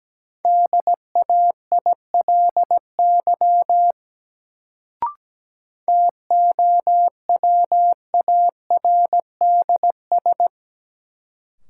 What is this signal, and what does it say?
Morse code, heard as a keyed tone, about 17 wpm, 700 Hz. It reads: DAILY TOWARDS
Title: Sentences from Top 1100 Words - New Word Drill Alt - 17wpm